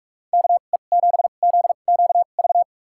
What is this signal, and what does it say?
Morse code, heard as a keyed tone, about 44 wpm, spaced as for 32 wpm, 700 Hz. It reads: KE7ZQV
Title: USA Call Signs - 6 Characters - 2x Character Spacing - 45wpm